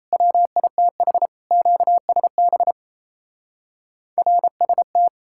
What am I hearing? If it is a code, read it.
Morse code, heard as a keyed tone, about 36 wpm, 700 Hz. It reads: WITH QSB RST